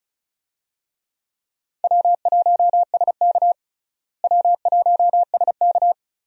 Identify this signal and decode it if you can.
Morse code, heard as a keyed tone, about 36 wpm, 700 Hz. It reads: W1SK W1SK
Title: USA Call Signs - 4 Characters - Repeated 3x - 35wpm